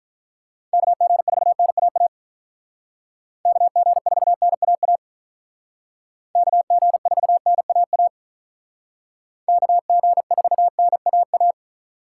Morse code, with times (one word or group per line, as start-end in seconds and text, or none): 0.7-2.1 KG4NAA
3.5-5.0 KG4NAA
6.4-8.1 KG4NAA
9.5-11.5 KG4NAA